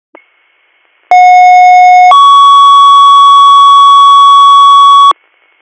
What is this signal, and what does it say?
Two-tone page, heard as a one-second tone, then a three-second tone, about 730 then 1120 Hz. It is heard from 1.1 to 5.1 s.